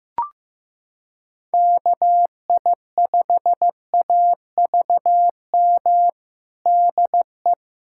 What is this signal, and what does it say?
Morse code, heard as a keyed tone, about 15 wpm, 700 Hz. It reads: KI5AVM DE
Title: USA Callsign DE Callsign - 6 Characters - 15wpm